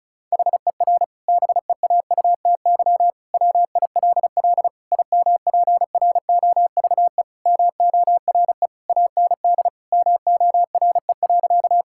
Morse code, heard as a keyed tone, about 36 wpm, 700 Hz. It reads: HER BEAUTY WILL IMPROVE MORE AND MORE.
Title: Sentences from Top 1000 Words - Review Part 5 - 35wpm